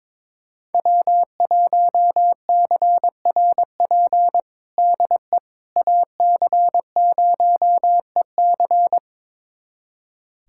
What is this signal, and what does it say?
Morse code, heard as a keyed tone, about 22 wpm, 700 Hz. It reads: W1CRP DE AC0EC
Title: USA Callsign DE Callsign - 5 Characters - 22wpm